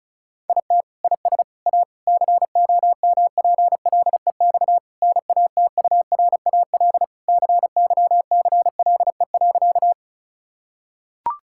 Morse code, heard as a keyed tone, about 36 wpm, 700 Hz. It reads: IT IS A COMPLEX NATURAL CYCLE.
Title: Sentences from Top 1000 Words - Review Part 6 - 35wpm